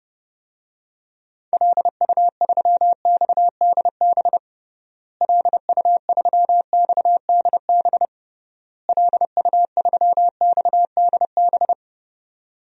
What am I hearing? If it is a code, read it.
Morse code, heard as a keyed tone, about 30 wpm, 700 Hz. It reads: LU3XDB LU3XDB LU3XDB
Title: DXCC-100 Call Signs - 6 Characters - Repeated 3x - 30wpm